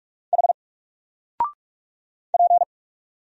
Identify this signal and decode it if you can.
Morse code, heard as a keyed tone, about 44 wpm, 700 Hz. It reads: H P
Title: Single Letter-Number 45wpm